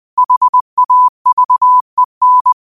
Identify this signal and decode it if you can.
Morse code, heard as a keyed tone, about 20 wpm, 1000 Hz. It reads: HAVEN